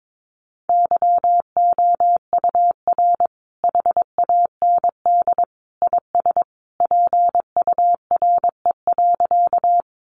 Morse code, heard as a keyed tone, about 22 wpm, 700 Hz. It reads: YOUR HAND IS PURE.
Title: Sentences from Top 1500 Words 22wpm